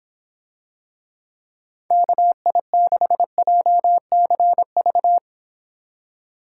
Morse code, heard as a keyed tone, about 26 wpm, 700 Hz. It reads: KI6JCV